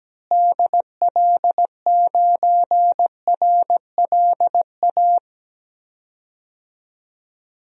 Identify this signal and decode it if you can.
Morse code, heard as a keyed tone, about 17 wpm, 700 Hz. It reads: DL9RLA
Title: DXCC-100 Call Signs - 6 Characters - Repeated 3x - 17wpm